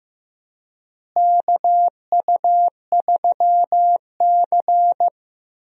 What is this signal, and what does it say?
Morse code, heard as a keyed tone, about 15 wpm, 700 Hz. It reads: KU3C
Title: USA Call Signs - 4 Characters - Repeated 3x - 15wpm